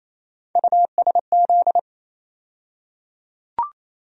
Morse code, heard as a keyed tone, about 28 wpm, 700 Hz. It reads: USZ
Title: USA Call Sign Postfixes - 3 Letters 28wpm